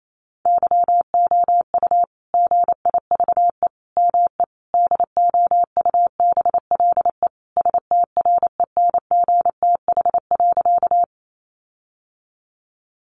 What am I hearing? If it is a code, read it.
Morse code, heard as a keyed tone, about 28 wpm, 700 Hz. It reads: YOU GIVE ME DOUBLE STRENGTH.
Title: Sentences from Top 1400 Words 28wpm